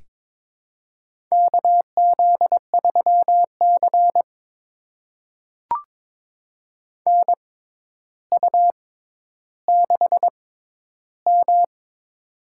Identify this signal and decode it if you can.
Morse code, heard as a keyed tone, about 22 wpm, spaced as for 11 wpm, 700 Hz. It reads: KZ3C NU6M
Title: USA Call Signs - 4 Characters - 6x Character Spacing - 22wpm